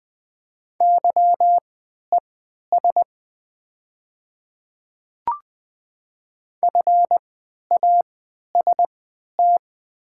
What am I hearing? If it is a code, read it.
Morse code, heard as a keyed tone, about 20 wpm, spaced as for 11 wpm, 700 Hz. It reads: YES FAST
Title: 3x Character Spacing - Top 500 Words - 20wpm